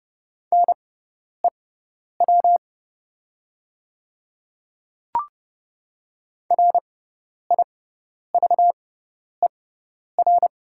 Morse code, heard as a keyed tone, about 30 wpm, spaced as for 10 wpm, 700 Hz. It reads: NEW RIVER
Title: 6x Character Spacing - Top 300 Words - 30wpm